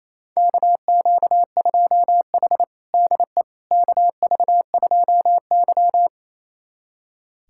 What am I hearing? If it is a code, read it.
Morse code, heard as a keyed tone, about 28 wpm, 700 Hz. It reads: KQ2H DE KV2Y
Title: USA Callsign DE Callsign - 4 Characters - 28wpm